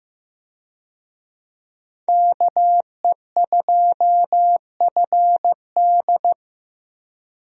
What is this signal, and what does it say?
Morse code, heard as a keyed tone, about 15 wpm, 700 Hz. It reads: KE2FD
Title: USA Call Signs - 5 Characters - Repeated 2x - 15wpm